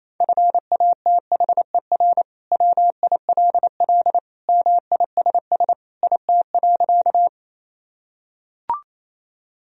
Morse code, heard as a keyed tone, about 28 wpm, 700 Hz. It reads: FATHER WILL MISS IT.
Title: Sentences from Top 1000 Words - Review Part 3 - 28wpm